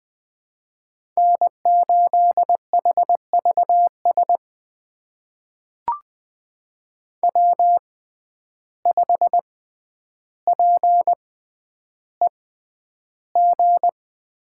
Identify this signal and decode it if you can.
Morse code, heard as a keyed tone, about 20 wpm, spaced as for 10 wpm, 700 Hz. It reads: N8HVS W5PEG
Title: USA Call Signs - 5 Characters - 6x Character Spacing - 20wpm